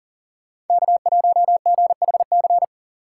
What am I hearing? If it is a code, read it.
Morse code, heard as a keyed tone, about 49 wpm, 700 Hz. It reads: K1GHC
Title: Speed Racing - USA 1x3 Call Signs 40wpm